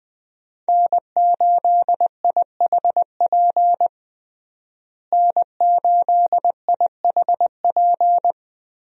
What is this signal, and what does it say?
Morse code, heard as a keyed tone, about 20 wpm, 700 Hz. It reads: N8IHP N8IHP
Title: USA Call Signs - 5 Characters - Repeated 3x - 20wpm